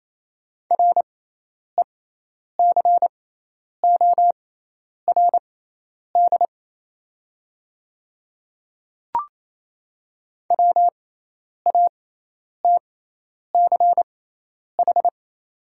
Morse code, heard as a keyed tone, about 28 wpm, spaced as for 10 wpm, 690 Hz. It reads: RECORD WATCH
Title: 6x Character Spacing - Top 500 Words - 28wpm